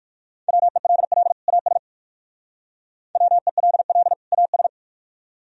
Morse code, heard as a keyed tone, about 59 wpm, 700 Hz. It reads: WELL AS WELL AS